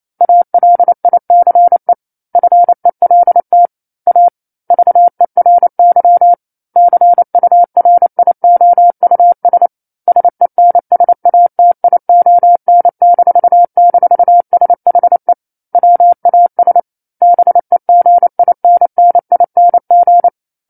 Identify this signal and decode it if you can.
Morse code, heard as a keyed tone, about 28 wpm, 700 Hz. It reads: ALICE FELT A VERY CURIOUS SENSATION--SHE WAS BEGINNING